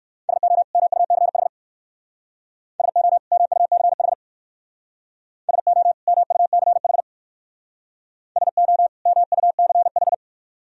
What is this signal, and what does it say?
Morse code, heard as a keyed tone, about 56 wpm, 700 Hz. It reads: SO MUCH SO MUCH SO MUCH SO MUCH
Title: Speed-Racing - Sets of 2 Words 45wpm